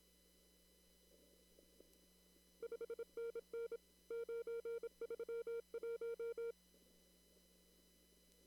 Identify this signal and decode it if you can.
Morse code, heard as a keyed tone, about 22 wpm, 460 Hz. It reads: 5NN 931
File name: CW / kt5j wpx cw-op-k5pi